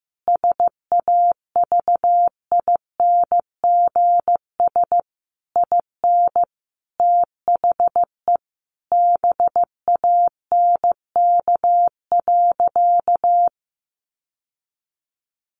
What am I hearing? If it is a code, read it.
Morse code, heard as a keyed tone, about 15 wpm, 700 Hz. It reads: SAVINGS IN THE BANK.
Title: Sentences from Top 1200 Words 15wpm